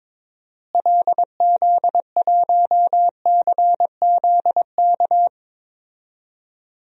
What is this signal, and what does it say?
Morse code, heard as a keyed tone, about 22 wpm, 700 Hz. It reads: LZ1CZK
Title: DXCC-100 Call Signs - 6 Characters - Repeated 3x - 22wpm